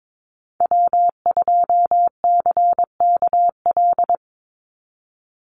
Morse code, heard as a keyed tone, about 22 wpm, 700 Hz. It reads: W2CKL
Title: NA SOTA Activator Callsigns 22wpm